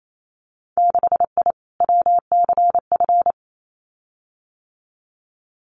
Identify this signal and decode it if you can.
Morse code, heard as a keyed tone, about 28 wpm, 700 Hz. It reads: 6I WCF
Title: Winter Field Day Exchanges Encore 28wpm